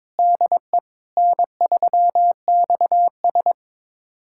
Morse code, heard as a keyed tone, about 22 wpm, 700 Hz. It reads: DE N3XS
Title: USA Callsign DE Callsign - 4 Characters - 22wpm